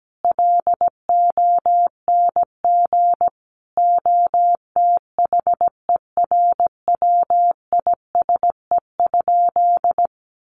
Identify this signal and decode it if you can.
Morse code, heard as a keyed tone, about 17 wpm, 700 Hz. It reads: LONG OTHERWISE?
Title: Sentences from Top 1200 Words 17wpm